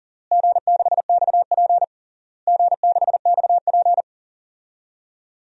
Morse code, heard as a keyed tone, about 40 wpm, 700 Hz. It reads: G6XP G6XP
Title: DXCC-100 Call Signs - 4 Characters - Repeated 3x - 40wpm